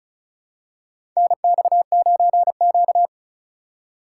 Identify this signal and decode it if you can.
Morse code, heard as a keyed tone, about 35 wpm, 690 Hz. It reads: NX9Q
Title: USA Call Signs - 4 Characters - 2x Character Spacing - 35wpm